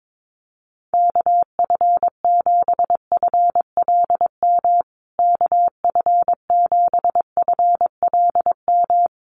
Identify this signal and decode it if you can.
Morse code, heard as a keyed tone, about 22 wpm, 700 Hz. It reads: KF7FLM KF7FLM
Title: USA Call Signs Repeated 3x 22wpm